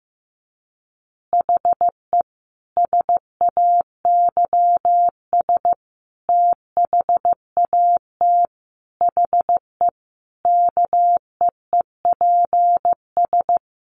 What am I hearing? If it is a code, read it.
Morse code, heard as a keyed tone, about 15 wpm, 700 Hz. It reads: HE SAYS THAT HE KEEPS
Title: Additional Easy Phrases 15wpm